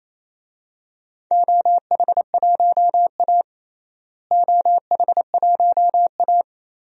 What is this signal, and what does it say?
Morse code, heard as a keyed tone, about 28 wpm, 700 Hz. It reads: OH1A OH1A
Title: DXCC-100 Call Signs - 4 Characters - Repeated 3x - 28wpm